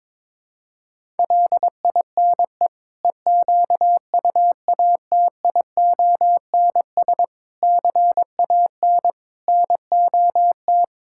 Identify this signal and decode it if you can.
Morse code, heard as a keyed tone, about 22 wpm, 700 Hz. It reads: LINE EQUATIONS CAN NOT
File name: practice / 22wpm / Build A Phrase - 5-Word Phrases - 22wpm